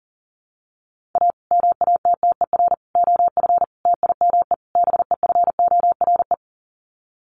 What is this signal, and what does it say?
Morse code, heard as a keyed tone, about 40 wpm, 700 Hz. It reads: A MATTER OF TIME BEFORE
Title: Sets of 5 Words Encore 40wpm